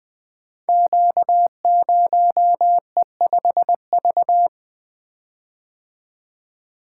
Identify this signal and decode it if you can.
Morse code, heard as a keyed tone, about 20 wpm, 700 Hz. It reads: Q0E5V
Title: Code Groups - 5 Letter:Numbers - 20wpm